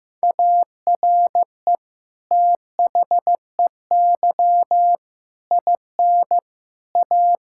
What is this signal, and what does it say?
Morse code, heard as a keyed tone, about 15 wpm, 700 Hz. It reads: ARE THEY IN A
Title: Sentences from Top 1000 Words - Review Part 6 - 15wpm